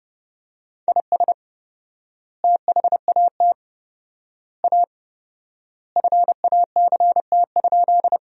Morse code, heard as a keyed tone, about 32 wpm, 700 Hz. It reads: IS THAT A FACT?
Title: 4x Word-Spacing - Sentences from Top 500 Words - 30wpm